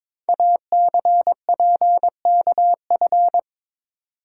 Morse code, heard as a keyed tone, about 22 wpm, 700 Hz. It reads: ACPKF
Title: Code Groups - 5 Letter:Numbers - 22wpm